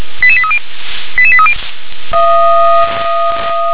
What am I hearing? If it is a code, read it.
Five-tone selective call (ZVEI), heard as quick runs of five tones, tones of about 70 ms. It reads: 88830 88830
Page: /357/SDR/ZVEI/88743/